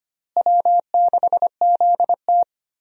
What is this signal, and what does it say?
Morse code, heard as a keyed tone, about 25 wpm, 700 Hz. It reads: W6ZT
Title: USA Call Signs - 4 Characters - Repeated 2x - 25wpm